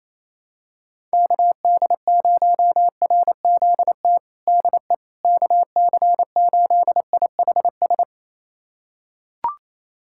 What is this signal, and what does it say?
Morse code, heard as a keyed tone, about 28 wpm, 700 Hz. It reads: KD0RZT DE KC8IHS
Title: USA Callsign DE Callsign 28wpm